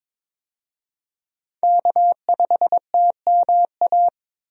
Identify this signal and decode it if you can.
Morse code, heard as a keyed tone, about 22 wpm, 700 Hz. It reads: K5TMA
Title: USA Call Signs - 5 Characters - 22wpm